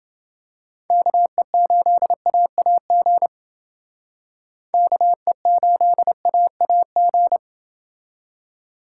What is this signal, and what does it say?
Morse code, heard as a keyed tone, about 28 wpm, 700 Hz. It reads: KE8AAG KE8AAG